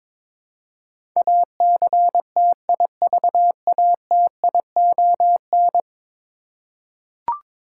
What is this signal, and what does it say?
Morse code, heard as a keyed tone, about 23 wpm, 700 Hz. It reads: ACTIVATION
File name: practice / 22wpm / 10 Letter Words - First 3 Letters Given - 22wpm